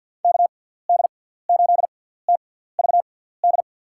Morse code, heard as a keyed tone, about 47 wpm, spaced as for 17 wpm, 700 Hz. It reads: KD8TVD
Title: USA Call Signs - 6 Characters - 6x Character Spacing - 50wpm